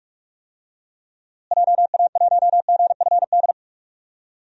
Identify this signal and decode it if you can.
Morse code, heard as a keyed tone, about 46 wpm, 700 Hz. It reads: JA1GFD
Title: DXCC-100 Call Signs - 6 Characters - Repeated 3x - 45wpm